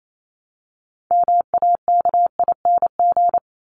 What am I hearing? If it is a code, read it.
Morse code, heard as a keyed tone, about 28 wpm, 700 Hz. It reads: MAKING
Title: ING Suffix - Top 100 Words with Base Words 28wpm